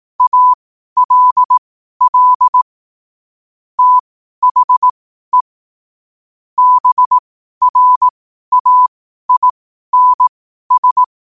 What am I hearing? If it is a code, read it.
Morse code, heard as a keyed tone, about 18 wpm, spaced as for 12 wpm, 1000 Hz. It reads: ALL THE BRAINS
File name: qotdfiles / 12wpm